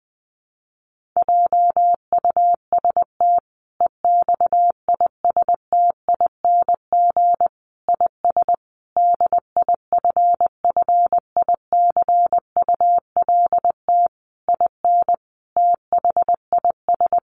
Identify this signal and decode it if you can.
Morse code, heard as a keyed tone, about 20 wpm, 700 Hz. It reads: JUST EXISTING IS DIFFICULT IN THIS
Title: Sentences from Top 1500 Words 20wpm